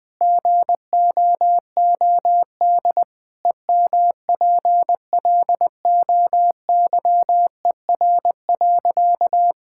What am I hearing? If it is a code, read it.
Morse code, heard as a keyed tone, about 21 wpm, 700 Hz. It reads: GOOD EMPLOYER.
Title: Sentences from Top 1600 Words 20wpm